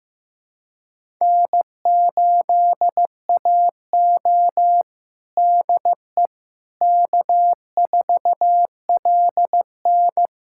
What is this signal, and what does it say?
Morse code, heard as a keyed tone, about 16 wpm, 700 Hz. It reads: N8AO DE K4LN
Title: USA Callsign DE Callsign - 4 Characters - 15wpm